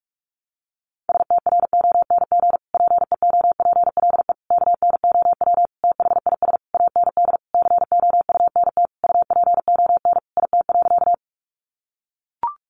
Morse code, heard as a keyed tone, about 44 wpm, 700 Hz. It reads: STRONG PEOPLE KNOW THIS AND COUNT UPON IT.
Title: Sentences from Top 1500 Words 45wpm